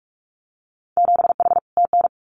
Morse code, heard as a keyed tone, about 46 wpm, 700 Hz. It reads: 7H TN
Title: Winter Field Day Exchanges Encore 45wpm